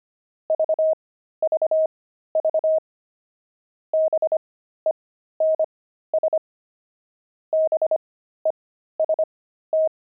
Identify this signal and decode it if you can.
Morse code, heard as a keyed tone, about 25 wpm, spaced as for 13 wpm, 630 Hz. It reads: VVV BENS BEST
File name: pangram-fs-13-cs-25-wire